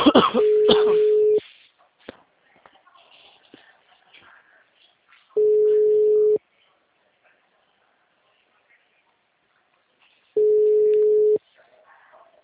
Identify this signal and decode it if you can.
Ringback tone (Europe, 425 Hz), with speech.
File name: out- -2022- - - .0